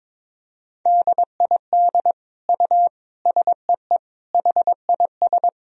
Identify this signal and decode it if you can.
Morse code, heard as a keyed tone, about 22 wpm, 700 Hz. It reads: DID U SEE HIS
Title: Sentences from Top 300 Words and Abbreviations 22wpm